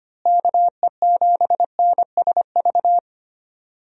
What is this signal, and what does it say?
Morse code, heard as a keyed tone, about 25 wpm, 700 Hz. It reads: KE7NSV
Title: USA Call Signs - 6 Characters - Repeated 2x - 25wpm